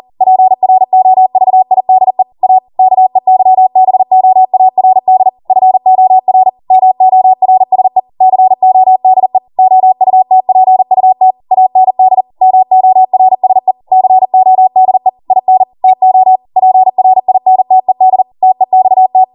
Morse code, heard as a keyed tone, about 40 wpm, 750 Hz. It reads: PROVIDE A KEYBOARD FOR MORSE CODE OUTPUT AND MORSE CODE IN TO PRINTED TEXT